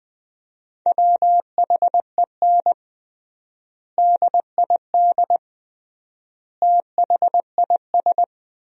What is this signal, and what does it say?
Morse code, heard as a keyed tone, about 20 wpm, 700 Hz. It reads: WHEN DID THIS
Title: 3x Word-Spacing - Sentences from Top 400 Words - 20wpm